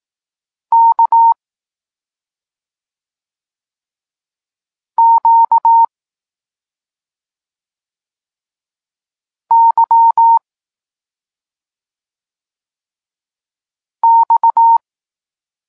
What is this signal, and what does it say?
Morse code, heard as a keyed tone, about 18 wpm, 920 Hz. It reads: K Q Y X
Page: Tekst 32 – Doskonalenie odbioru poznanych liter